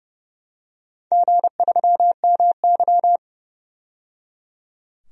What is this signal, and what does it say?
Morse code, heard as a keyed tone, about 31 wpm, 700 Hz. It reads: G3MY
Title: DXCC-100 Call Signs - 4 Characters - 30wpm